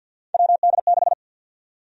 Morse code, heard as a keyed tone, about 49 wpm, 700 Hz. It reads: WD6
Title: USA Call Sign Prefixes - 50wpm